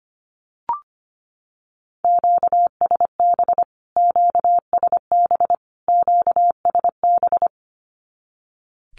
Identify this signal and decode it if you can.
Morse code, heard as a keyed tone, about 25 wpm, 700 Hz. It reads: QSB QSB QSB
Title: Q-Codes Repeated 3x 25wpm